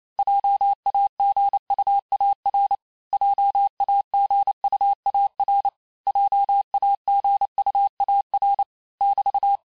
Morse code, heard as a keyed tone, about 28 wpm, 800 Hz. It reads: JAGUAR JAGUAR JAGUAR <BT>
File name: cours-du-mercredi / audios / cours-865-teste-les-mots-vitesse-27mpm